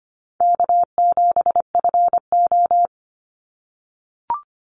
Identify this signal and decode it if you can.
Morse code, heard as a keyed tone, about 25 wpm, 700 Hz. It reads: K7FO
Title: NA SOTA Chaser Callsigns 25wpm